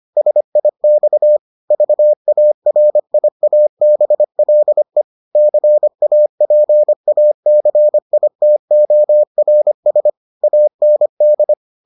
Morse code, heard as a keyed tone, about 25 wpm, 600 Hz. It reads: SIX VARIABLE CAPACITORS AND